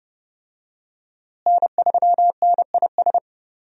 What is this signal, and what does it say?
Morse code, heard as a keyed tone, about 28 wpm, 700 Hz. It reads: N3NIS
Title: USA Call Signs - 5 Characters - Repeated 3x - 30wpm